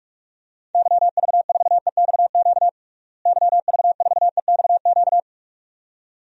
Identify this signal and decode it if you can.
Morse code, heard as a keyed tone, about 46 wpm, 700 Hz. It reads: YV4EXQ YV4EXQ
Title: DXCC-100 Call Signs - 6 Characters - Repeated 2x - 45wpm